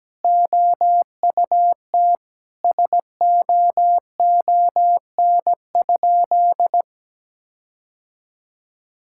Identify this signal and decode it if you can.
Morse code, heard as a keyed tone, about 17 wpm, 700 Hz. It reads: OUT SOON?
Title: Sentences from Top 1600 Words 17wpm